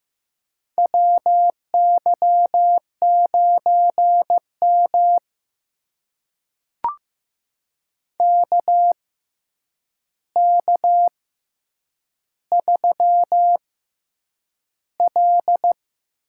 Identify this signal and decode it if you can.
Morse code, heard as a keyed tone, about 15 wpm, spaced as for 8 wpm, 700 Hz. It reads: WY9M KK3L
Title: USA Call Signs - 4 Characters - 6x Character Spacing - 15wpm